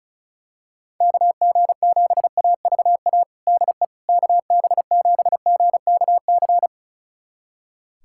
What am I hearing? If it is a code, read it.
Morse code, heard as a keyed tone, about 36 wpm, 700 Hz. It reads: KG7AVA DE KB7GKC
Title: USA Callsign DE Callsign - 6 Characters - 35wpm